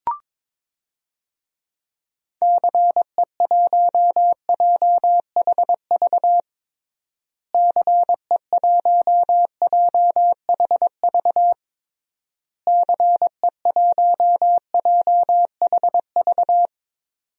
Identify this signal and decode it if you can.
Morse code, heard as a keyed tone, about 22 wpm, 700 Hz. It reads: CE1JHV CE1JHV CE1JHV
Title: DXCC-100 Call Signs - 6 Characters - Repeated 3x - 22wpm